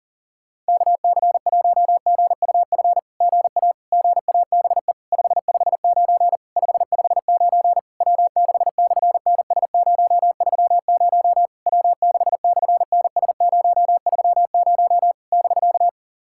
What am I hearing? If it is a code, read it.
Morse code, heard as a keyed tone, about 37 wpm, 690 Hz. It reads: KC1GUF GA GABE 559 559 W6/NS030 W6/NS030 <BK>
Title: SOTA QSO Practice 40wpm